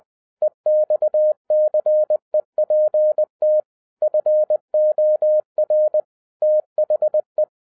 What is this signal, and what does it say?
Morse code, heard as a keyed tone, about 20 wpm, 610 Hz. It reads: EXCEPT FOR THE